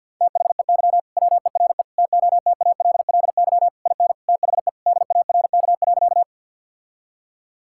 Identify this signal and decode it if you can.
Morse code, heard as a keyed tone, about 50 wpm, 700 Hz. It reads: THEY WERE TOTALLY IN THE DARK.
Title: Sentences from Top 1600 Words 50wpm